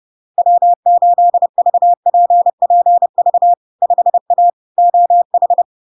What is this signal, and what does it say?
Morse code, heard as a keyed tone, about 30 wpm, 700 Hz. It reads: W8VPPV 5A OH
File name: LARK-FD-2022-Log-in-CW-30wpm-700-Hz